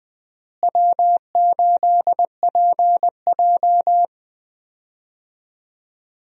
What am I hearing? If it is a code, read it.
Morse code, heard as a keyed tone, about 20 wpm, 700 Hz. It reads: W8PJ
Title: USA Call Signs - 4 Characters - Repeated 3x - 20wpm